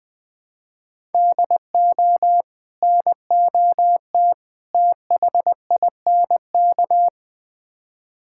Morse code, heard as a keyed tone, about 21 wpm, 700 Hz. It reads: DO NOT THINK
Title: Build A Phrase - 4-Word Phrases - 20wpm